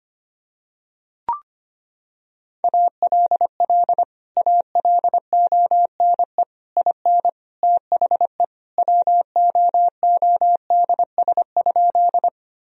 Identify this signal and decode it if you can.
Morse code, heard as a keyed tone, about 25 wpm, 700 Hz. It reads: ALL ALONE IN THE WOODS?
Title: Sentences from Top 1000 Words - Review Part 2 - 25wpm